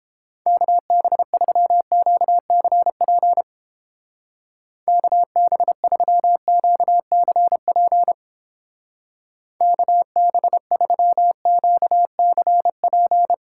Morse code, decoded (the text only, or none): KB3QCP KB3QCP KB3QCP